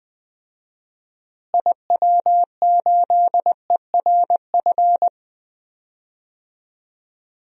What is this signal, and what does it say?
Morse code, heard as a keyed tone, about 20 wpm, 700 Hz. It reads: IW8ERF
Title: DXCC-100 Call Signs - 6 Characters - Repeated 3x - 20wpm